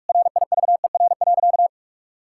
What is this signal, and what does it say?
Morse code, heard as a keyed tone, about 46 wpm, 700 Hz. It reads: RIVER.
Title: Sentences from Top 1000 Words - Review Part 5 - 45wpm